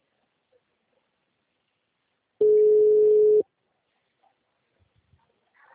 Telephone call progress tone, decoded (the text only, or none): ringback tone